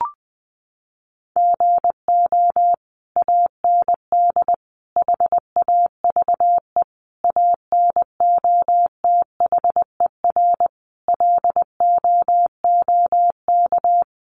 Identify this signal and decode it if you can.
Morse code, heard as a keyed tone, about 20 wpm, 700 Hz. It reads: GO AND HAVE ANOTHER LOOK
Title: Sentences from Top 600 Words 20wpm